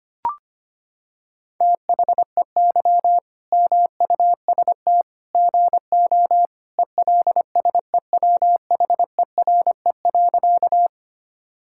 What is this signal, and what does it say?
Morse code, heard as a keyed tone, about 25 wpm, 700 Hz. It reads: THEY MUST GO ELSEWHERE.
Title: Sentences from Top 1600 Words 25wpm